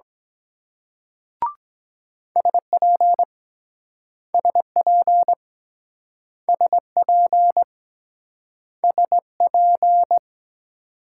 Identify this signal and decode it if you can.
Morse code, heard as a keyed tone, about 23 wpm, 700 Hz. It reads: SP SP SP SP